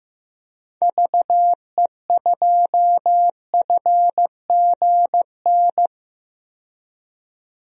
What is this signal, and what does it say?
Morse code, heard as a keyed tone, about 15 wpm, 700 Hz. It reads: VE2FGN
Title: DXCC-100 Call Signs - 6 Characters - 15wpm